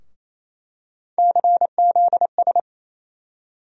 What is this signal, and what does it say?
Morse code, heard as a keyed tone, about 28 wpm, 700 Hz. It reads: CZS
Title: USA Call Sign Postfixes - 3 Letters 28wpm